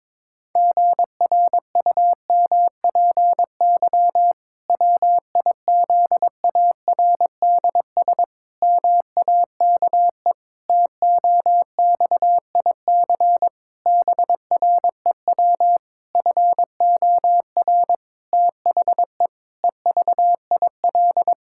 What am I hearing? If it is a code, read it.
Morse code, heard as a keyed tone, about 22 wpm, 700 Hz. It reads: GRUMPY WIZARDS MAKE TOXIC BREW FOR THE EVIL
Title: Pangram Challenge v2 - 22wpm